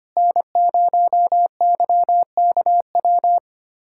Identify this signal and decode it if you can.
Morse code, heard as a keyed tone, about 25 wpm, 700 Hz. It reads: N0YKW
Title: USA Callsign DE Callsign 25wpm